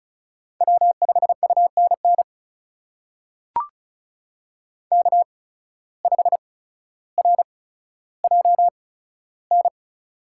Morse code, heard as a keyed tone, about 34 wpm, spaced as for 15 wpm, 700 Hz. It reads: W5UNN K5RJN
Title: USA Call Signs - 5 Characters - 8x Character Spacing - 35wpm